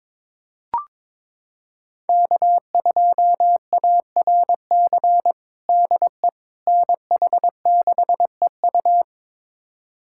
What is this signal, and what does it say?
Morse code, heard as a keyed tone, about 22 wpm, 700 Hz. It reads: K2ARC DE NH6EU
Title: USA Callsign DE Callsign - 5 Characters - 22wpm